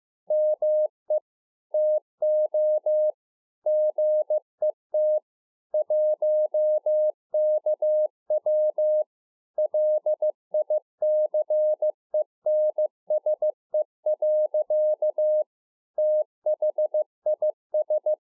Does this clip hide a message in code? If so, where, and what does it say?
Yes: Morse code from 0.3 to 18.1 s, heard as a keyed tone, about 15 wpm, 610 Hz. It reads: ME TO GET 1KW LICENSE. THIS